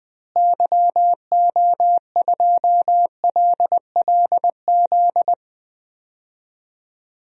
Morse code, heard as a keyed tone, about 20 wpm, 700 Hz. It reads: YO2LLZ
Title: DXCC-100 Call Signs - 6 Characters - Repeated 2x - 20wpm